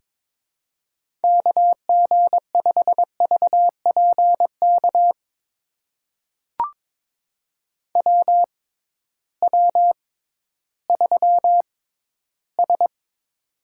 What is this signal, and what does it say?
Morse code, heard as a keyed tone, about 22 wpm, spaced as for 13 wpm, 700 Hz. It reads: KG5VPK WW3S
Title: USA Call Signs - 6x Character Spacing - 22wpm